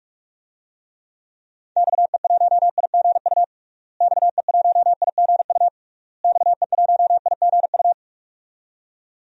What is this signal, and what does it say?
Morse code, heard as a keyed tone, about 46 wpm, 700 Hz. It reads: XE1IGU XE1IGU XE1IGU
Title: DXCC-100 Call Signs - 6 Characters - Repeated 3x - 45wpm